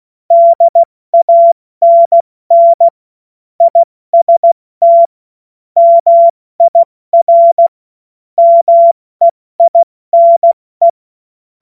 Morse code, heard as a keyed tone, about 16 wpm, 680 Hz. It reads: DANN IST MIR MEINE
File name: Morsekurs / Witze / Witz 22. Februar 2022 70 F80